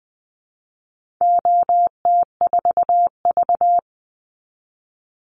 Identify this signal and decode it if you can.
Morse code, heard as a keyed tone, about 20 wpm, 700 Hz. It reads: OT4V
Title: Call Signs Repeated 3x - 20wpm